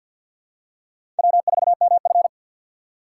Morse code, heard as a keyed tone, about 52 wpm, 700 Hz. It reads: W4MF
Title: USA Call Signs - 4 Characters - 2x Character Spacing - 50wpm